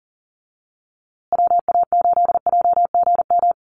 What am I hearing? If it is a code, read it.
Morse code, heard as a keyed tone, about 40 wpm, 700 Hz. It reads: WA8JGM
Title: USA Call Signs Repeated 2x 40wpm